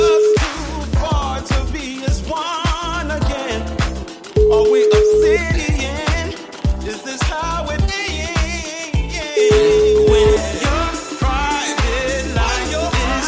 Ringback tone (Europe, 425 Hz), with noise.